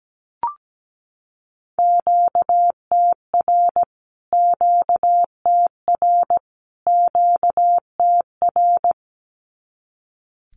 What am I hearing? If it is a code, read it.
Morse code, heard as a keyed tone, about 17 wpm, 700 Hz. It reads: QTR QTR QTR
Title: Q-Codes Repeated 3x 17wpm